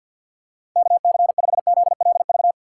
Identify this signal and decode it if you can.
Morse code, heard as a keyed tone, about 51 wpm, 700 Hz. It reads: KC5ZLV
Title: USA Call Signs - 6 Characters - 4x Character Spacing - 50wpm